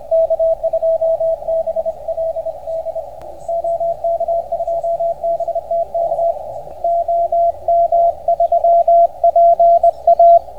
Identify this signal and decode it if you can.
Morse code, heard as a keyed tone, about 28 wpm, 660 Hz, with radio noise. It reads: K2BLD OK2BTK OM3PA